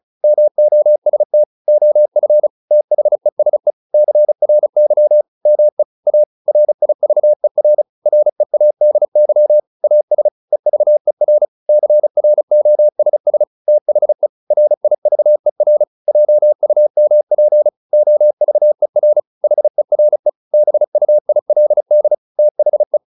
Morse code, heard as a keyed tone, about 35 wpm, 600 Hz. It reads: MOST OF THESE CRY ME A RIVER READY AS EVER CROSS THE RIVER JUMP OVER HERE BUILD THE